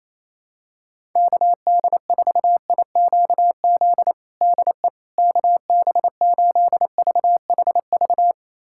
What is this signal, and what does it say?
Morse code, heard as a keyed tone, about 28 wpm, 700 Hz. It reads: KD4IQZ DE KB8VHV
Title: USA Callsign DE Callsign - 6 Characters - 28wpm